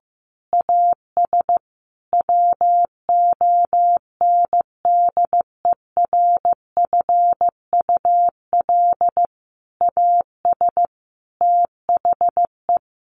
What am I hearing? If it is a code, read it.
Morse code, heard as a keyed tone, about 15 wpm, 700 Hz. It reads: AS WONDERFUL AS THE
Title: Sentences from Top 1400 Words 15wpm